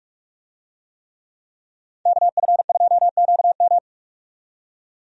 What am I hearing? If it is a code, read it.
Morse code, heard as a keyed tone, about 46 wpm, 700 Hz. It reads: KF2QM
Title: USA Call Signs - 5 Characters - 45wpm